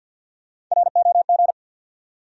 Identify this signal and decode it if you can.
Morse code, heard as a keyed tone, about 52 wpm, 700 Hz. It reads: ROG
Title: USA Common Names - Max 3 Letters - 50wpm